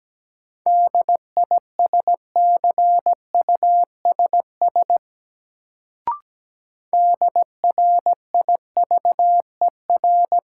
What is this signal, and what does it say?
Morse code, heard as a keyed tone, about 17 wpm, 700 Hz. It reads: DISCUSS DRIVER
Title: Sentences from Top 1000 Words - New Word Drill - 17wpm